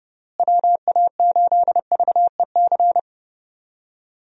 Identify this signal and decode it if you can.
Morse code, heard as a keyed tone, about 29 wpm, 700 Hz. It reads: WA8VEC
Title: USA Call Signs - 6 Characters - Repeated 2x - 30wpm